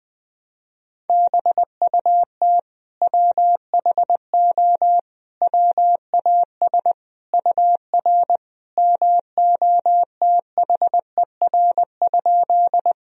Morse code, heard as a keyed tone, about 20 wpm, 700 Hz. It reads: BUT WHO WAS UR MOTHER?
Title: Sentences from Top 200 Words and Abbreviations 20wpm